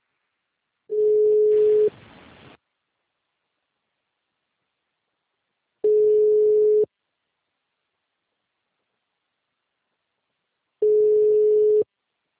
Ringback tone (Europe, 425 Hz).